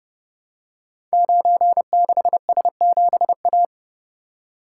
Morse code, heard as a keyed tone, about 30 wpm, 700 Hz. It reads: 96S7A
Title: Code Groups - 5 Letter:Numbers - 30wpm